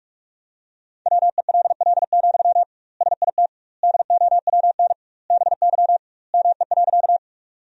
Morse code, heard as a keyed tone, about 41 wpm, 700 Hz. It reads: WELL, SIT DOWN BY ME.
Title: 2x Word-Spacing - Sentences from Top 500 Words - 45wpm